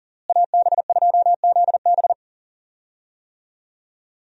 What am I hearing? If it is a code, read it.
Morse code, heard as a keyed tone, about 37 wpm, 700 Hz. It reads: AB2ZB
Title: USA Call Signs - 5 Characters - Repeated 3x - 40wpm